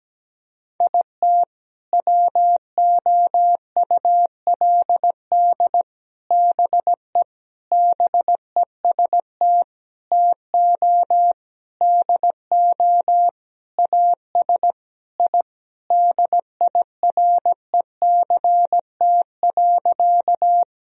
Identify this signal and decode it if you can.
Morse code, heard as a keyed tone, about 17 wpm, 700 Hz. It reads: IT WOULD BE BEST TO DO AS I DIRECT.
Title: Sentences from Top 1000 Words - Review Part 2 - 17wpm